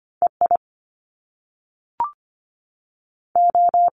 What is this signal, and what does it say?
Morse code, heard as a keyed tone, about 25 wpm, spaced as for 3 wpm, 700 Hz. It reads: E IO
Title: Top Two Letter Combinations in Words 25wpm